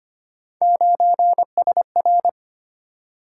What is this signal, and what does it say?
Morse code, heard as a keyed tone, about 25 wpm, 700 Hz. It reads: 9SR
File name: Three Letter-Number Combinations - 25wpm